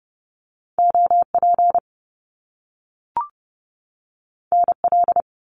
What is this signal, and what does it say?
Morse code, heard as a keyed tone, about 30 wpm, 700 Hz. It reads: OP NL
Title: Top Two Letter Combinations in Words 30wpm